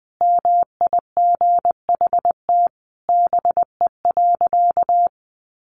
Morse code, heard as a keyed tone, about 19 wpm, 700 Hz. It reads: MIGHT BE.
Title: Sentences from Top 1500 Words 20wpm